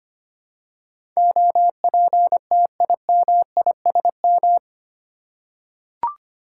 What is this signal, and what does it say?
Morse code, heard as a keyed tone, about 25 wpm, 700 Hz. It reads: OPTIMISM
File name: Eight Letter Words - 25wpm